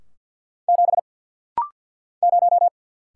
Morse code, heard as a keyed tone, about 50 wpm, 700 Hz. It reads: 7 0
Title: Single Letter-Number - Rapid-Fire 50wpm